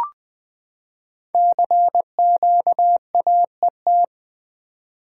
Morse code, heard as a keyed tone, about 20 wpm, 700 Hz. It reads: CQAET
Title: Code Groups - 5 Letter:Numbers - 20wpm